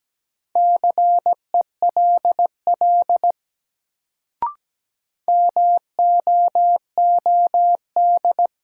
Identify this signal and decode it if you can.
Morse code, heard as a keyed tone, about 17 wpm, 700 Hz. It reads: CELL MOOD
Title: Sentences from Top 1100 Words - New Word Drill Alt - 17wpm